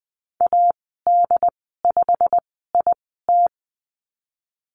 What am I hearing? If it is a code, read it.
Morse code, heard as a keyed tone, about 20 wpm, spaced as for 15 wpm, 700 Hz. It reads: AD5IT
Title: 147 Lesson 24g - Letter y - Call Signs 20wpm